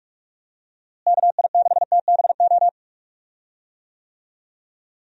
Morse code, heard as a keyed tone, about 47 wpm, 700 Hz. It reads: KI6TBO
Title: USA Call Signs - 6 Characters - Repeated 3x - 45wpm